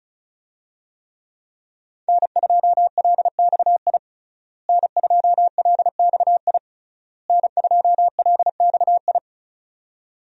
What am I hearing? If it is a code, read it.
Morse code, heard as a keyed tone, about 36 wpm, 700 Hz. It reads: N2LXI N2LXI N2LXI
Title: USA Call Signs - 5 Characters - Repeated 3x - 35wpm